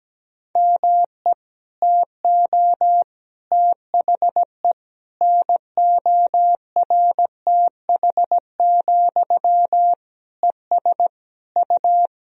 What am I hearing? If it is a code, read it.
Morse code, heard as a keyed tone, about 17 wpm, 700 Hz. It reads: ME TO THE NORTH, ES U
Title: Sentences from Top 400 Words and Abbreviations 17wpm